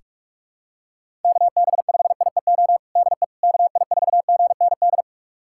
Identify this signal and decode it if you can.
Morse code, heard as a keyed tone, about 48 wpm, 700 Hz. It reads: KB5IEO DE KI4GND
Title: USA Callsign DE Callsign 45wpm